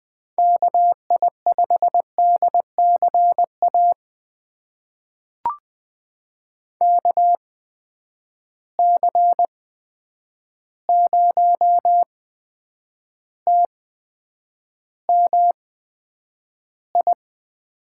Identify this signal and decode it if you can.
Morse code, heard as a keyed tone, about 20 wpm, spaced as for 9 wpm, 700 Hz. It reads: KI5DCA KC0TMI
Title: USA Call Signs - 8x Character Spacing - 20wpm